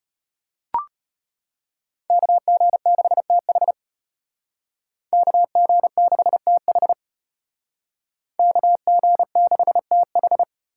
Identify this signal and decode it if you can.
Morse code, heard as a keyed tone, about 33 wpm, 700 Hz. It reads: KG6TH KG6TH KG6TH